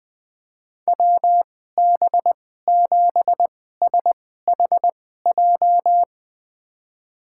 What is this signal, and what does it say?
Morse code, heard as a keyed tone, about 20 wpm, spaced as for 14 wpm, 700 Hz. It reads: WB7SHJ
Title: USA Call Signs - 6 Characters - 2x Character Spacing - 20wpm